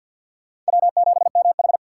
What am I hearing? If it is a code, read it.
Morse code, heard as a keyed tone, about 48 wpm, 700 Hz. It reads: W7MH
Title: USA Call Signs - 4 Characters - 6x Character Spacing - 50wpm